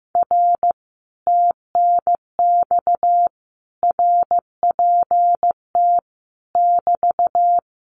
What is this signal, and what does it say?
Morse code, heard as a keyed tone, about 15 wpm, 700 Hz. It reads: R TNX RPT <BT>
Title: Common QSO Phrases Encore 15wpm